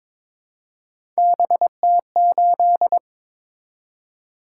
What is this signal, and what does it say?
Morse code, heard as a keyed tone, about 22 wpm, 700 Hz. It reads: BT8
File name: practice / 22wpm / Three Letter-Number Combinations - 22wpm